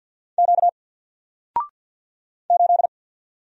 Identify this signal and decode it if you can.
Morse code, heard as a keyed tone, about 48 wpm, 700 Hz. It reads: Q 8
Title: Single Letter-Number Alt 50wpm